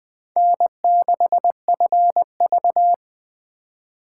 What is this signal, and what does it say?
Morse code, heard as a keyed tone, about 20 wpm, 700 Hz. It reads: N6FV
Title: Speed Racing - USA 1x2 Call Signs 17wpm